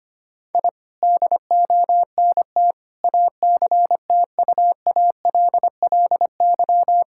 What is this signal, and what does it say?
Morse code, heard as a keyed tone, about 25 wpm, 700 Hz. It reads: I DONT ACTUALLY
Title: Sentences from Top 1000 Words - Review Part 5 - 25wpm